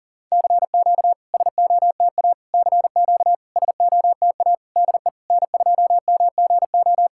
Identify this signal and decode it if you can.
Morse code, heard as a keyed tone, about 40 wpm, 700 Hz. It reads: CQ SOTA CQ SOTA DE N2MGO
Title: SOTA QSO Practice 40wpm